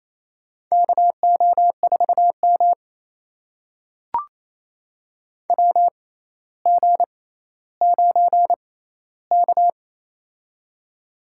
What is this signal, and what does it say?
Morse code, heard as a keyed tone, about 28 wpm, spaced as for 14 wpm, 700 Hz. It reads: KO4M WG9K
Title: USA Call Signs - 6x Character Spacing - 28wpm